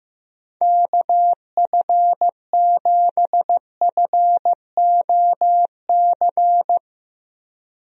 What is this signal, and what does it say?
Morse code, heard as a keyed tone, about 15 wpm, 700 Hz. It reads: KF7FOC
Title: USA Call Signs - 6 Characters - Repeated 2x - 15wpm